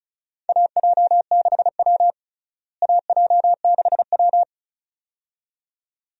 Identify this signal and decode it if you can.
Morse code, heard as a keyed tone, about 34 wpm, 700 Hz. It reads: AJ6W AJ6W
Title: USA Call Signs - 4 Characters - Repeated 2x - 35wpm